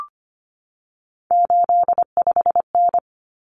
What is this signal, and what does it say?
Morse code, heard as a keyed tone, about 25 wpm, 700 Hz. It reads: 85N
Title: Common Cut Numbers - 3-Digit 25wpm